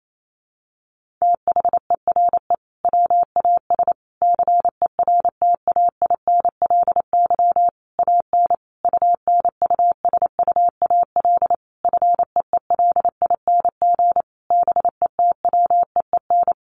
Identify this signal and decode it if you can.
Morse code, heard as a keyed tone, about 28 wpm, 700 Hz. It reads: THERE WAS CERTAINLY AN UNUSUAL FEELING BETWEEN
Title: Sentences from Top 1400 Words 28wpm